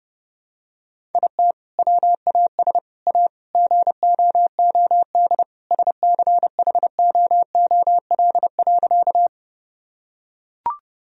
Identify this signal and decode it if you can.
Morse code, heard as a keyed tone, about 30 wpm, 700 Hz. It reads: IT WAS A GOOD SCHOOL.
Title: Sentences from Top 1000 Words - Review Part 1 - 30wpm